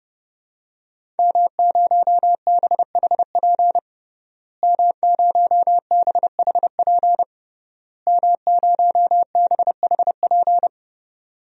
Morse code, heard as a keyed tone, about 32 wpm, 700 Hz. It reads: M0BHP M0BHP M0BHP
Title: DXCC-100 Call Signs - 5 Characters - Repeated 3x - 30wpm